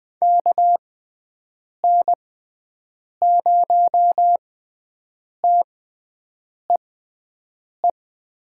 Morse code, heard as a keyed tone, about 21 wpm, spaced as for 7 wpm, 700 Hz. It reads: KN0TEE
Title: USA Call Signs - 6 Characters - 6x Character Spacing - 20wpm